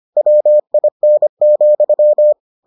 Morse code, heard as a keyed tone, about 25 wpm, 600 Hz. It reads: WIN,